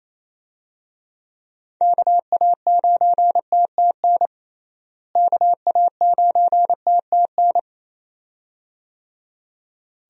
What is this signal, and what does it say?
Morse code, heard as a keyed tone, about 28 wpm, 700 Hz. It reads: KA9TTN KA9TTN
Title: USA Call Signs - 6 Characters - Repeated 2x - 28wpm